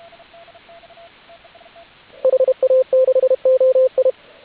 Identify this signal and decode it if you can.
Morse code, heard as a keyed tone, about 32 wpm, 520 Hz, with radio noise. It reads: HA6OI
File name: AUDIO2019 / HG8DX CW